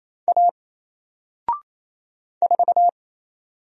Morse code, heard as a keyed tone, about 27 wpm, 700 Hz. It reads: A 4
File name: Single-Letter-Number-28wpm